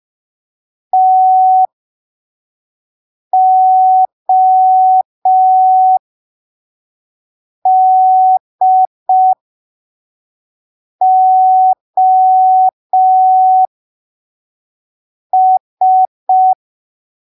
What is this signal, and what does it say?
Morse code, heard as a keyed tone, about 5 wpm, spaced as for 3 wpm, 750 Hz. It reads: TODOS
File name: arquivos / texto / texto 02 5ppm